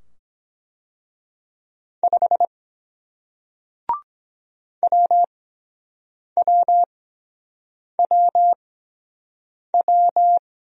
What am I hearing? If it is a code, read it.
Morse code, heard as a keyed tone, about 23 wpm, spaced as for 7 wpm, 700 Hz. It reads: 5 WWWW